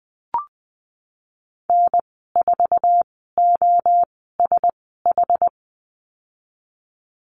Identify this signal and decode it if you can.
Morse code, heard as a keyed tone, about 20 wpm, spaced as for 14 wpm, 700 Hz. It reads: N4OSH
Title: 062 Lesson 11g - Letter h - Call Signs 20wpm